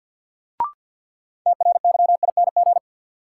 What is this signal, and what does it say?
Morse code, heard as a keyed tone, about 53 wpm, 700 Hz. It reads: TRYING
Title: Sets of 2 Words - New Word Drill - 50wpm